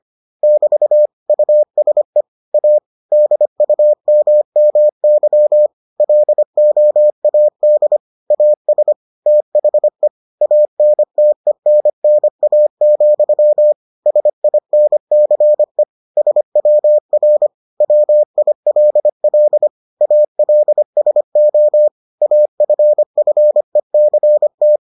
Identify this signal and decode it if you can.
Morse code, heard as a keyed tone, about 25 wpm, 600 Hz. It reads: <BT> USE A DUMMY LOAD AS THE ANTENNA, SINCE SWR WILL ALSO AFFECT